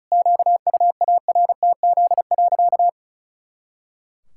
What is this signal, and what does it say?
Morse code, heard as a keyed tone, about 36 wpm, 700 Hz. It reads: QUARTZ.
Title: Pangram Challenge v2 - 35wpm